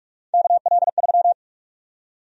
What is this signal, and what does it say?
Morse code, heard as a keyed tone, about 41 wpm, 700 Hz. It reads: KL3
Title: USA Call Sign Prefixes - 45wpm